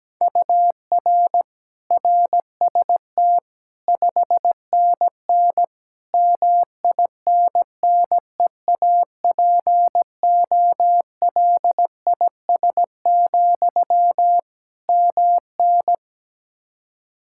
Morse code, heard as a keyed tone, about 17 wpm, 700 Hz. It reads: UR RST 5NN MINNEAPOLIS, MN
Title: Practice QSO - Exchange 1 - Beginning - 17wpm